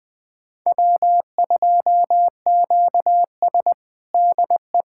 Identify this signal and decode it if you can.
Morse code, heard as a keyed tone, about 20 wpm, 700 Hz. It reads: W2QS DE
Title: USA Callsign DE Callsign - 4 Characters - 20wpm